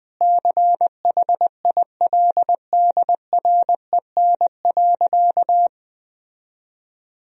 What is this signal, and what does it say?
Morse code, heard as a keyed tone, about 20 wpm, 700 Hz. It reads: CHILDREN.
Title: Sentences from Top 1000 Words - Review Part 3 - 20wpm